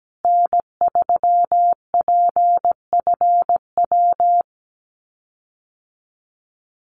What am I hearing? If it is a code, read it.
Morse code, heard as a keyed tone, about 17 wpm, 700 Hz. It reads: N3PFW
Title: USA Call Signs Repeated 3x 17wpm